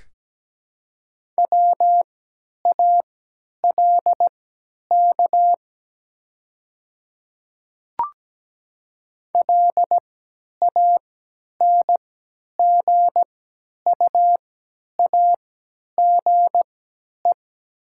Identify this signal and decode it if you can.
Morse code, heard as a keyed tone, about 17 wpm, spaced as for 10 wpm, 700 Hz. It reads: WALK LANGUAGE
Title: 3x Character Spacing - Top 500 Words - 17wpm